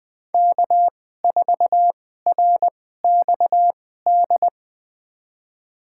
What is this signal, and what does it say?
Morse code, heard as a keyed tone, about 20 wpm, spaced as for 15 wpm, 700 Hz. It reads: K4RXD
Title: USA Call Signs - 5 Characters - 2x Character Spacing - 20wpm